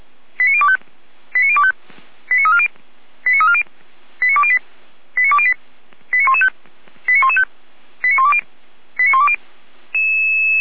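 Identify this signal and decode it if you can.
Five-tone selective call (ZVEI), heard as quick runs of five tones, tones of about 70 ms. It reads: 89025 89025 89340 89340 89208 89208 89115 89115 89120 89120
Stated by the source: ZVEI five-tone selective call received by SDR.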